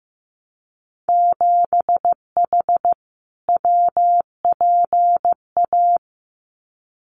Morse code, heard as a keyed tone, about 15 wpm, 700 Hz. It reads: 7H WPA
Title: Winter Field Day Exchanges Encore 15wpm